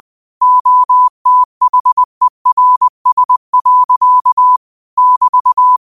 Morse code, heard as a keyed tone, about 20 wpm, 1000 Hz. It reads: OTHERS. <BT>